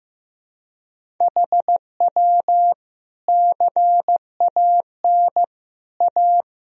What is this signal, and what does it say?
Morse code, heard as a keyed tone, about 16 wpm, 700 Hz. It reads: HW CAN A
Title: Sentences from Top 500 Words and Abbreviations 15wpm